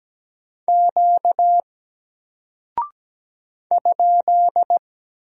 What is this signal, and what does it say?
Morse code, heard as a keyed tone, about 17 wpm, 700 Hz. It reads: Q ?
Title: Single Character (Letter:Number:Punctuation:Prosign) Alt 17wpm